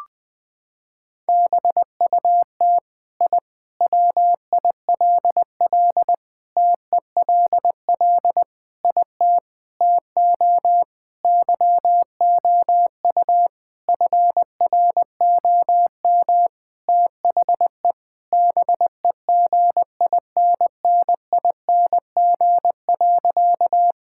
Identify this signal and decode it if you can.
Morse code, heard as a keyed tone, about 20 wpm, 700 Hz. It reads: BUT I WILL TELL IT TO YOU FROM THE BEGINNING.
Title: Sentences from Top 1000 Words - Review Part 5 - 20wpm